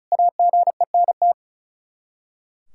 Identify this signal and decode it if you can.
Morse code, heard as a keyed tone, about 34 wpm, 700 Hz. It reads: AGENT
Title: K1USN SST QSO Practice 35wpm